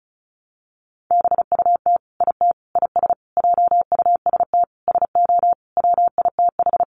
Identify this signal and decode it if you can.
Morse code, heard as a keyed tone, about 34 wpm, 700 Hz. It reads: BUT IT IS JUST SO WITH
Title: Sentences from Top 200 Words 35wpm